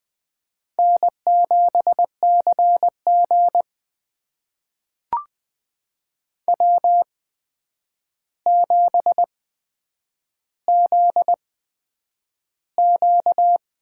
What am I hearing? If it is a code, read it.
Morse code, heard as a keyed tone, about 20 wpm, spaced as for 9 wpm, 700 Hz. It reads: N7CG W7ZQ
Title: USA Call Signs - 8x Character Spacing - 20wpm